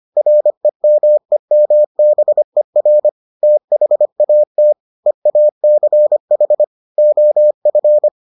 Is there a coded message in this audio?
Yes: Morse code from 0.2 to 8.1 s, heard as a keyed tone, about 25 wpm, 600 Hz. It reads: REMEMBER THAT EACH OF